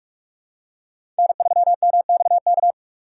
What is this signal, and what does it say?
Morse code, heard as a keyed tone, about 43 wpm, 700 Hz. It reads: N3MXK
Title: USA Call Signs - 5 Characters - 2x Character Spacing - 45wpm